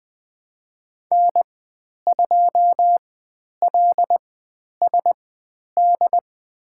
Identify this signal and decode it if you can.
Morse code, heard as a keyed tone, about 20 wpm, spaced as for 10 wpm, 700 Hz. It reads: N2LSD
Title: 095 Lesson 16g - Letter c - Call Signs 20wpm